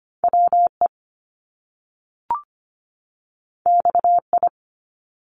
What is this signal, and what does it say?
Morse code, heard as a keyed tone, about 25 wpm, 700 Hz. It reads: WE XI
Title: Top Two Letter Combinations in Words 25wpm